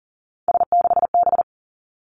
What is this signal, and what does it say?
Morse code, heard as a keyed tone, about 42 wpm, 700 Hz. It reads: S6B
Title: Three Challenging Letter-Number Combinations 40wpm